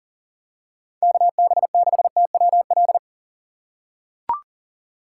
Morse code, heard as a keyed tone, about 40 wpm, 700 Hz. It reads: KB6TWL
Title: USA Call Signs - 6 Characters - 3x Character Spacing - 40wpm